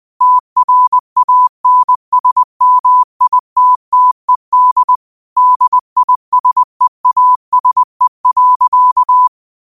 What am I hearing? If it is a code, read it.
Morse code, heard as a keyed tone, about 20 wpm, 1000 Hz. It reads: TRANSMITTED DISEASE.
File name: qotdfiles / 20wpm